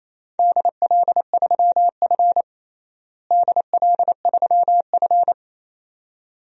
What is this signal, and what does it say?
Morse code, heard as a keyed tone, about 28 wpm, 700 Hz. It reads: DL3F DL3F
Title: DXCC-100 Call Signs - 4 Characters - Repeated 2x - 28wpm